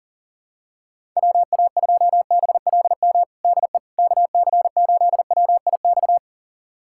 Morse code, heard as a keyed tone, about 40 wpm, 700 Hz. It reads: WA2BLM DE KC8WIX
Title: USA Callsign DE Callsign 40wpm